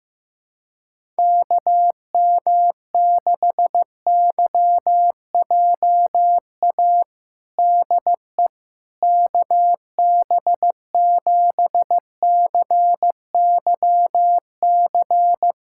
Morse code, heard as a keyed tone, about 15 wpm, 700 Hz. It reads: KM6YJA DE KB7CYC
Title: USA Callsign DE Callsign - 6 Characters - 15wpm